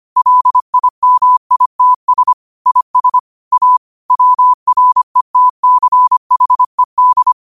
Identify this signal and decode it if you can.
Morse code, heard as a keyed tone, about 25 wpm, 1000 Hz. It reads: LIMITS IS A WRETCHED
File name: qotdfiles / 25wpm